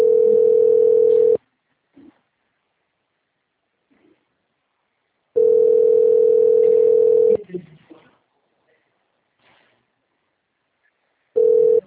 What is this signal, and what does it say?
Ringback tone (North America), 440 + 480 Hz.